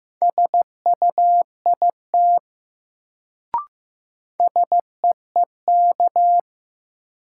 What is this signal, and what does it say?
Morse code, heard as a keyed tone, about 15 wpm, 700 Hz. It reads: SUIT SEEK
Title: Sentences from Top 1600 Words - New Word Drill - 15wpm